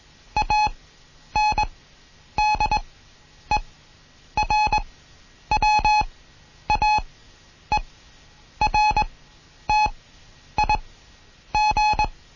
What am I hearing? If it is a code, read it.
Morse code, heard as a keyed tone, about 21 wpm, spaced as for 9 wpm, 850 Hz. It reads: ANDERWAERTIG
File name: Morsen / Kurs A 20